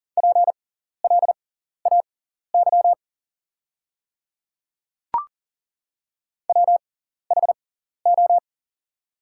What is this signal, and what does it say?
Morse code, heard as a keyed tone, about 40 wpm, spaced as for 14 wpm, 690 Hz. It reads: PLAY WHO
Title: 6x Character Spacing - Top 200 Words - 40wpm